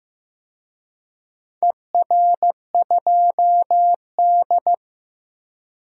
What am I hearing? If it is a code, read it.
Morse code, heard as a keyed tone, about 15 wpm, 700 Hz. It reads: ER2D
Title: DXCC-100 Call Signs - 4 Characters - 15wpm